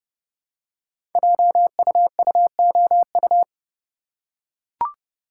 Morse code, heard as a keyed tone, about 29 wpm, 700 Hz. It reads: JUUOU
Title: Code Groups - 5 Letters - 30wpm